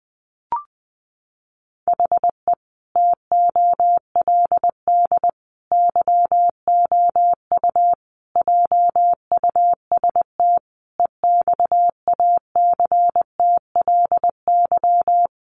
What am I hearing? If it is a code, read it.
Morse code, heard as a keyed tone, about 20 wpm, 700 Hz. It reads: HE TOLD YOU JUST EXACTLY
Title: Sentences from Top 900 Words 20wpm